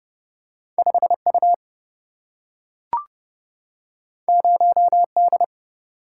Morse code, heard as a keyed tone, about 30 wpm, 700 Hz. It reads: 5U 0D
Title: Single-Letter Single-Number 30wpm